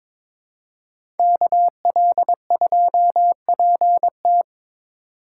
Morse code, heard as a keyed tone, about 22 wpm, 700 Hz. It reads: KL2PT
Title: USA Call Signs - 5 Characters - 4x Character Spacing - 22wpm